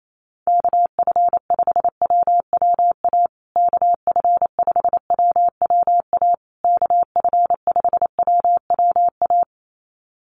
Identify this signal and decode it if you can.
Morse code, heard as a keyed tone, about 28 wpm, 700 Hz. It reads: KF5WWA KF5WWA KF5WWA
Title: USA Call Signs Repeated 3x 28wpm